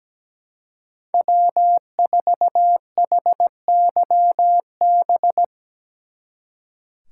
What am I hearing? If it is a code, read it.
Morse code, heard as a keyed tone, about 18 wpm, 700 Hz. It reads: W4HYB
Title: Speed Racing - USA 1x3 Call Signs 17wpm